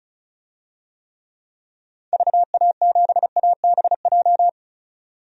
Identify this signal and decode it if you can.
Morse code, heard as a keyed tone, about 36 wpm, 700 Hz. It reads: VA7ABJ
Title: DXCC-100 Call Signs - 6 Characters - 35wpm